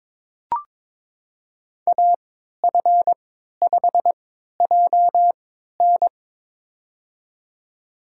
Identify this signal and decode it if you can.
Morse code, heard as a keyed tone, about 23 wpm, spaced as for 13 wpm, 700 Hz. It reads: AF5JN
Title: USA Call Signs - 5 Characters - 3x Character Spacing - 22wpm